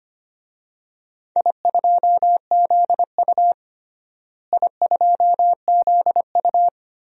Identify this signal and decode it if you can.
Morse code, heard as a keyed tone, about 25 wpm, 700 Hz. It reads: I2ZU I2ZU
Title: DXCC-100 Call Signs - 4 Characters - Repeated 3x - 25wpm